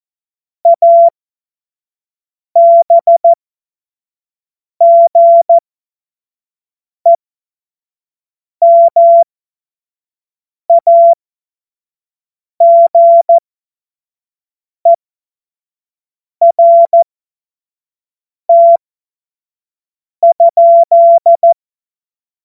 Morse code, heard as a keyed tone, about 14 wpm, spaced as for 5 wpm, 680 Hz. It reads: ABGEMAGERT?
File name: Witz-23-September-2025-25-F70